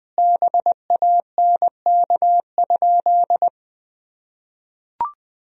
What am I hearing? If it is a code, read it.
Morse code, heard as a keyed tone, about 21 wpm, 700 Hz. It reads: BANK?
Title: Sentences from Top 1000 Words - Review Part 4 - 20wpm